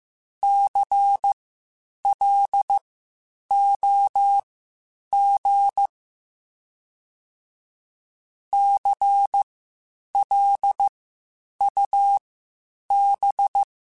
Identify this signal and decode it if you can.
Morse code, heard as a keyed tone, about 15 wpm, spaced as for 8 wpm, 800 Hz. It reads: CLOG CLUB
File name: Spaced Alphas / Spaced Alphas mp3s / Cs sp @13